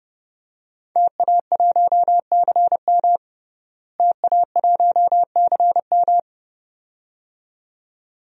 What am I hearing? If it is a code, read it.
Morse code, heard as a keyed tone, about 32 wpm, 700 Hz. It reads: TA1CM TA1CM
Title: DXCC-100 Call Signs - 5 Characters - Repeated 2x - 30wpm